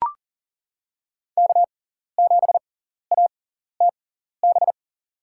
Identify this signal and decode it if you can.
Morse code, heard as a keyed tone, about 40 wpm, spaced as for 14 wpm, 700 Hz. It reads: K7ATB
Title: USA Call Signs - 5 Characters - 6x Character Spacing - 40wpm